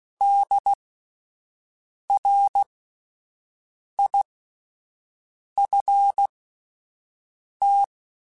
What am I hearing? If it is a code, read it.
Morse code, heard as a keyed tone, about 16 wpm, spaced as for 5 wpm, 800 Hz. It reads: DRIFT